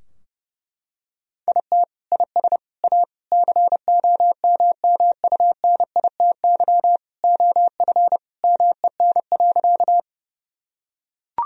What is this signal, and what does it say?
Morse code, heard as a keyed tone, about 30 wpm, 700 Hz. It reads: IT IS A COMMUNITY OF MEN.
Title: Sentences from Top 1000 Words - Review Part 4 - 30wpm